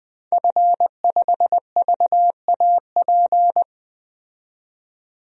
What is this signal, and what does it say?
Morse code, heard as a keyed tone, about 21 wpm, 700 Hz. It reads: F5VAP
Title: DXCC-100 Call Signs - 5 Characters - 20wpm